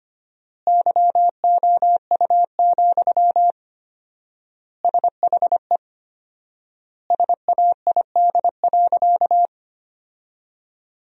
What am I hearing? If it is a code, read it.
Morse code, heard as a keyed tone, about 25 wpm, 700 Hz. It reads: YOU, SHE SAID.
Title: 4x Word-Spacing - Sentences from Top 400 Words - 25wpm